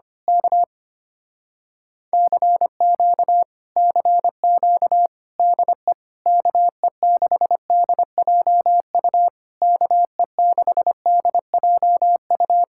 Morse code, heard as a keyed tone, about 24 wpm, 700 Hz. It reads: K CQ CQ DE KE6DJU KE6DJU
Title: Speed-Racing Common QSO Phrases Encore - 25wpm